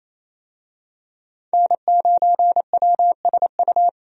Morse code, heard as a keyed tone, about 28 wpm, 700 Hz. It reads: N9WSU
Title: USA Call Signs - 5 Characters - 28wpm